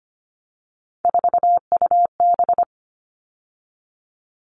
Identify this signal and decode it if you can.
Morse code, heard as a keyed tone, about 25 wpm, 700 Hz. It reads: 4UB
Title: All Cut Numbers - 3-Digit 25wpm